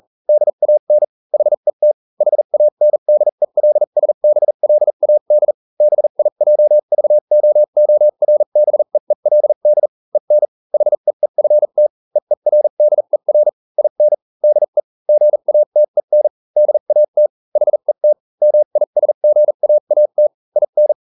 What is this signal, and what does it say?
Morse code, heard as a keyed tone, about 40 wpm, spaced as for 35 wpm, 600 Hz. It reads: DAN HET HANDELSBLAD BIJVOORBEELD EN HEEFT EERDER IN DE GATEN DAT HET MISGAAT IN